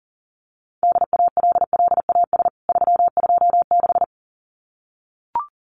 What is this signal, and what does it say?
Morse code, heard as a keyed tone, about 40 wpm, 700 Hz. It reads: DALLAS 326
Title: CWT Exchange 40wpm